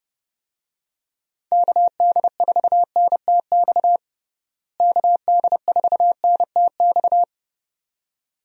Morse code, heard as a keyed tone, about 30 wpm, 700 Hz. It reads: KD4NTX KD4NTX
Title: USA Call Signs - 6 Characters - Repeated 2x - 30wpm